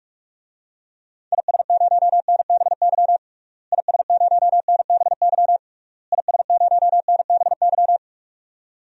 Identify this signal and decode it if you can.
Morse code, heard as a keyed tone, about 46 wpm, 700 Hz. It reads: IS0NBY IS0NBY IS0NBY
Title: DXCC-100 Call Signs - 6 Characters - Repeated 3x - 45wpm